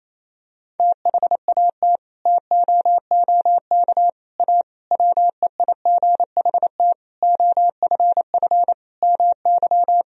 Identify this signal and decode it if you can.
Morse code, heard as a keyed tone, about 28 wpm, 700 Hz. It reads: THAT TOOK A WEIGHT OFF MY
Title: Sentences from Top 1000 Words - Review Part 3 - 28wpm